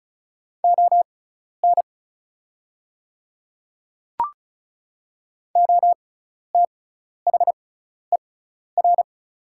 Morse code, heard as a keyed tone, about 36 wpm, spaced as for 12 wpm, 690 Hz. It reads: ON OTHER
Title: 6x Character Spacing - Top 100 Words - 35wpm